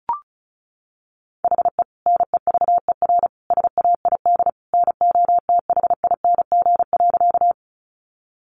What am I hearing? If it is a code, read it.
Morse code, heard as a keyed tone, about 34 wpm, 700 Hz. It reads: HE NEVER SAID NOTHING.
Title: Sentences from Top 500 Words 35wpm